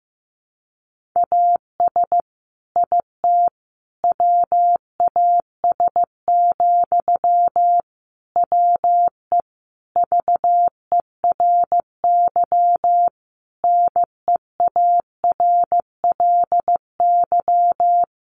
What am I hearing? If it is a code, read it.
Morse code, heard as a keyed tone, about 15 wpm, 700 Hz. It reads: AS IT WAS, WE VERY NEARLY
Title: Sentences from Top 1400 Words 15wpm